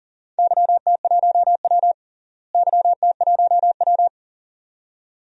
Morse code, heard as a keyed tone, about 40 wpm, 700 Hz. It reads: YT1W YT1W
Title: DXCC-100 Call Signs - 4 Characters - Repeated 2x - 40wpm